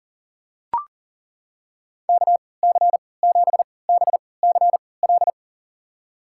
Morse code, heard as a keyed tone, about 40 wpm, spaced as for 23 wpm, 700 Hz. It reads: KC7BCL
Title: USA Call Signs - 6 Characters - 3x Character Spacing - 40wpm